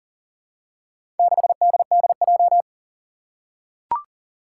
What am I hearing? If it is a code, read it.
Morse code, heard as a keyed tone, about 37 wpm, 700 Hz. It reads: 6DDJ
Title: Four Challenging Letter-Number Combinations 40wpm